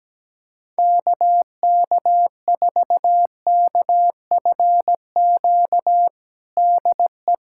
Morse code, heard as a keyed tone, about 17 wpm, 700 Hz. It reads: KK4KFQ DE
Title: USA Callsign DE Callsign - 6 Characters - 17wpm